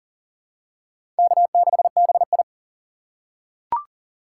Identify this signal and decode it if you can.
Morse code, heard as a keyed tone, about 40 wpm, 700 Hz. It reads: K6BI
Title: USA Call Signs - 3x Character Spacing - 40wpm